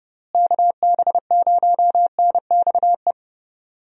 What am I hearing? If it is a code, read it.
Morse code, heard as a keyed tone, about 30 wpm, 700 Hz. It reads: KB0NXE
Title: USA Call Signs - 6 Characters - Repeated 3x - 30wpm